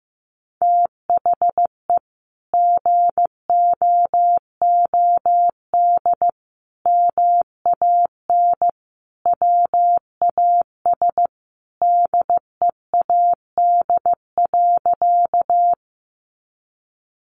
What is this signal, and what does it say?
Morse code, heard as a keyed tone, about 15 wpm, 700 Hz. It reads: THE GOOD MAN WAS DEAD.
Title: Sentences from Top 1500 Words 15wpm